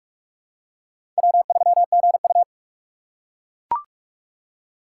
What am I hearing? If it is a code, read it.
Morse code, heard as a keyed tone, about 41 wpm, 700 Hz. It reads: W3GU
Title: USA Call Signs - 4x Character Spacing - 45wpm